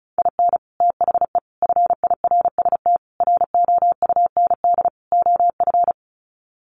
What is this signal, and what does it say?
Morse code, heard as a keyed tone, about 34 wpm, 700 Hz. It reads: IN THE FIRST ROUND OF
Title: Sets of 5 Words Encore 35wpm